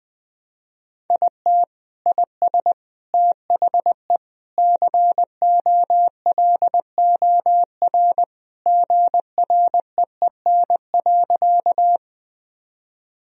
Morse code, heard as a keyed tone, about 20 wpm, 700 Hz. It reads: IT IS THE COLOR GREEN.
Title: Sentences from Top 1000 Words - Review Part 2 - 20wpm